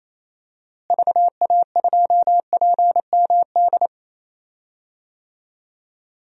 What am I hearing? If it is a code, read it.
Morse code, heard as a keyed tone, about 28 wpm, 700 Hz. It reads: VA2PMD
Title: DXCC-100 Call Signs - 6 Characters - Repeated 3x - 28wpm